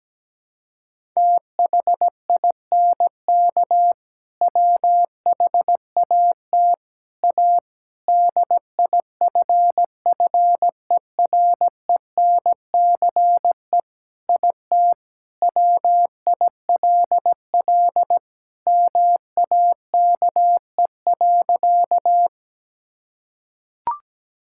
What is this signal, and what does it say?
Morse code, heard as a keyed tone, about 17 wpm, 700 Hz. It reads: THINK WHAT A DIFFERENCE IT WILL MAKE.
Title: Sentences from Top 1000 Words - Review Part 5 - 17wpm